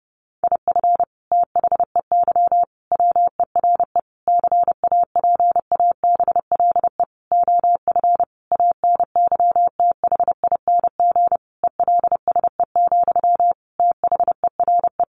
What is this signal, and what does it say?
Morse code, heard as a keyed tone, about 30 wpm, 700 Hz. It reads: IF THEY WERE CAPABLE OF ANYTHING ELSE, THERE
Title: Sentences from Top 1200 Words 30wpm